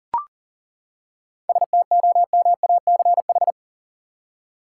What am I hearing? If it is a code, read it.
Morse code, heard as a keyed tone, about 40 wpm, 700 Hz. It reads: STOMACH
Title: Seven Letter Words - 40wpm